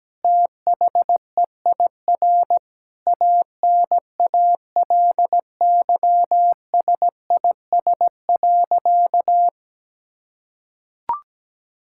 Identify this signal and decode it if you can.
Morse code, heard as a keyed tone, about 17 wpm, 700 Hz. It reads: THEIR ANALYSIS.
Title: Sentences from Top 1000 Words - Review Part 5 - 17wpm